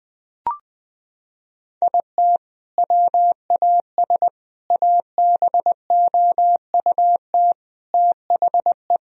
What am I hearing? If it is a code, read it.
Morse code, heard as a keyed tone, about 20 wpm, 700 Hz. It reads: IT WAS ABOUT THE
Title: Sentences from Top 1000 Words - Review Part 1 - 20wpm